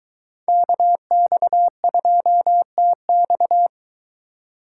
Morse code, heard as a keyed tone, about 23 wpm, 700 Hz. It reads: KX2TX